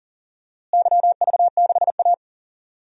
Morse code, heard as a keyed tone, about 40 wpm, 700 Hz. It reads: YV6A
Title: DXCC-100 Call Signs - 4 Characters - Repeated 2x - 40wpm